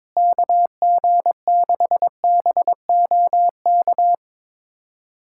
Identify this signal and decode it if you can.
Morse code, heard as a keyed tone, about 21 wpm, 700 Hz. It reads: KG6BOK
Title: USA Callsign DE Callsign - 6 Characters - 22wpm